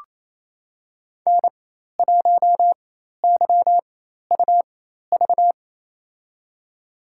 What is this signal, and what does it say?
Morse code, heard as a keyed tone, about 28 wpm, spaced as for 13 wpm, 700 Hz. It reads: N1YUV
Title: USA Call Signs - 4x Character Spacing - 28wpm